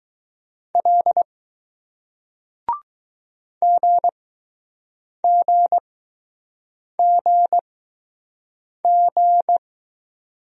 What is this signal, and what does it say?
Morse code, heard as a keyed tone, about 20 wpm, spaced as for 6 wpm, 700 Hz. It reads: L GGGG